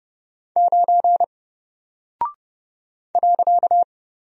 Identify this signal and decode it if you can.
Morse code, heard as a keyed tone, about 30 wpm, 700 Hz. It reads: 9 .
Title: Single Character (Letter:Number:Punctuation:Prosign) Alt 30wpm